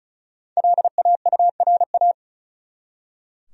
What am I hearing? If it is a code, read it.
Morse code, heard as a keyed tone, about 34 wpm, 700 Hz. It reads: LAURA
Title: USA Common Names - 35wpm